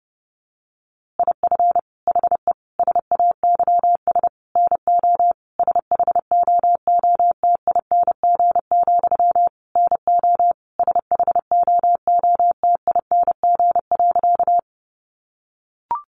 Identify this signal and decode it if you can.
Morse code, heard as a keyed tone, about 30 wpm, 700 Hz. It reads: IF HE SAYS NO SHOOTING, NO SHOOTING.
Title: A Sound Of Thunder 30wpm